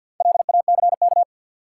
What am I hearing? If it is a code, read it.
Morse code, heard as a keyed tone, about 53 wpm, 700 Hz. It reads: LACK
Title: Sentences from Top 1100 Words - New Word Drill Alt - 50wpm